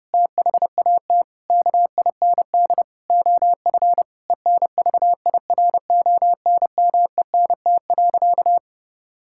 Morse code, heard as a keyed tone, about 30 wpm, 700 Hz. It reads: THAT KIND OF ENVIRONMENT.
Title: Sentences from Top 1000 Words - Review Part 5 - 30wpm